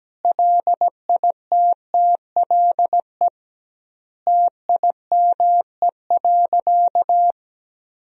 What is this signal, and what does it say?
Morse code, heard as a keyed tone, about 17 wpm, 700 Hz. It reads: LITTLE TIME.
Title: 2x Word-Spacing - Sentences from Top 200 Words - 17wpm